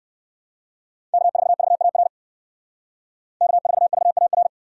Morse code, heard as a keyed tone, about 53 wpm, 700 Hz. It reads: K4VAR K4VAR
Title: Speed Racing - USA 1x3 Call Signs 45wpm